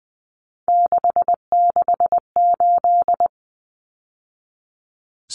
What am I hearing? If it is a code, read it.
Morse code, heard as a keyed tone, about 20 wpm, 700 Hz. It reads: 668
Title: Numbers - 3-Digit 20wpm